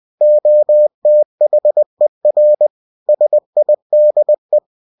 Morse code, heard as a keyed tone, about 20 wpm, 600 Hz. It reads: OTHER SIDE